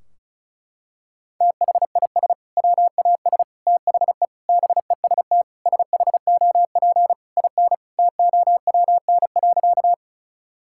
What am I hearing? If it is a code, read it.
Morse code, heard as a keyed tone, about 36 wpm, 700 Hz. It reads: THIS WAS THE BEST SHOP IN TOWN.
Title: Sentences from Top 1600 Words 35wpm